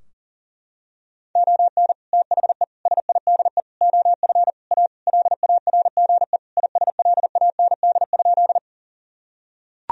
Morse code, heard as a keyed tone, about 40 wpm, 700 Hz. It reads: ON THE SIDE OF A LARGE ISLAND?
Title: Sentences from Top 1000 Words - Review Part 3 - 40wpm